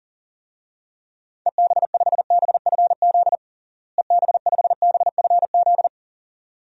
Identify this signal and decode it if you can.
Morse code, heard as a keyed tone, about 37 wpm, 700 Hz. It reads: EB5BFZ EB5BFZ
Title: DXCC-100 Call Signs - 6 Characters - Repeated 2x - 40wpm